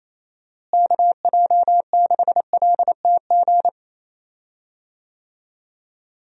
Morse code, heard as a keyed tone, about 28 wpm, 700 Hz. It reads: KJ6LTG
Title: USA Call Signs - 6 Characters - Repeated 2x - 28wpm